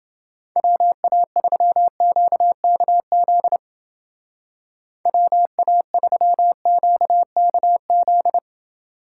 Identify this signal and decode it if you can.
Morse code, heard as a keyed tone, about 29 wpm, 700 Hz. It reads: WA3QKZ WA3QKZ